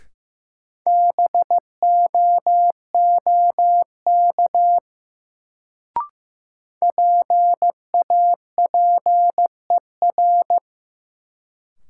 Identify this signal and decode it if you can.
Morse code, heard as a keyed tone, about 15 wpm, 700 Hz. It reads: BOOK PAPER
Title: Sentences from Top 300 Words - New Word Drill - 15wpm